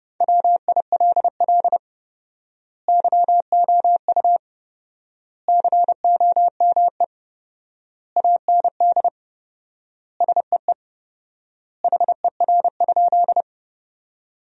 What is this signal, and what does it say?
Morse code, heard as a keyed tone, about 32 wpm, 700 Hz. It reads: WILL YOU COME AND SEE HER?
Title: 4x Word-Spacing - Sentences from Top 100 Words - 30wpm